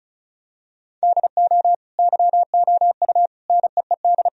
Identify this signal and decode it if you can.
Morse code, heard as a keyed tone, about 36 wpm, 700 Hz. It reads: DO YOU NEED
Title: Sentences from Top 1000 Words - Review Part 4 - 35wpm